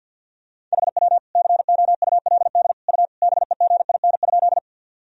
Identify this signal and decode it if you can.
Morse code, heard as a keyed tone, about 51 wpm, 700 Hz. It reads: HW COULD U BEGIN?
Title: Sentences from Top 300 Words and Abbreviations 50wpm